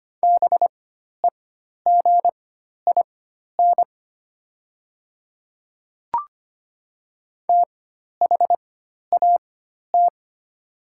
Morse code, heard as a keyed tone, about 25 wpm, spaced as for 12 wpm, 700 Hz. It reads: BEGIN THAT
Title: 4x Character Spacing - Top 300 Words - 25wpm